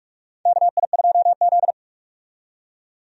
Morse code, heard as a keyed tone, about 46 wpm, 700 Hz. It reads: KI2Z
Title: USA Call Signs - 4 Characters - 45wpm